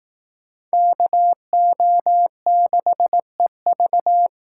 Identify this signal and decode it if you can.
Morse code, heard as a keyed tone, about 18 wpm, 700 Hz. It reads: KO6EV
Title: Speed Racing - USA 2x2 Call Signs 15wpm